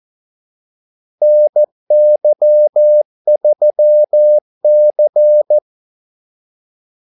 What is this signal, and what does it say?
Morse code, heard as a keyed tone, about 14 wpm, 600 Hz. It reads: NY3C
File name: Callsigns-14wpm